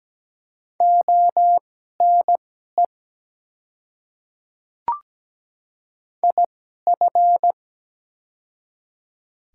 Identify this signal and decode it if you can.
Morse code, heard as a keyed tone, about 17 wpm, spaced as for 12 wpm, 690 Hz. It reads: ONE IF
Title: 2x Character Spacing - Top 100 Words - 17wpm